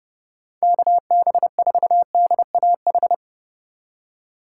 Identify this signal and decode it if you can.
Morse code, heard as a keyed tone, about 38 wpm, 700 Hz. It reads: KB4DAH